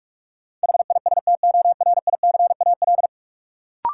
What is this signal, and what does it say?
Morse code, heard as a keyed tone, about 46 wpm, 700 Hz. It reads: HISTORICAL
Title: Sentences from Top 600 Words - New Word Drill - 45wpm